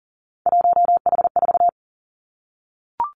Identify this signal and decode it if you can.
Morse code, heard as a keyed tone, about 38 wpm, 700 Hz. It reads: 1H4
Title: Three Challenging Letter-Number Combinations 40wpm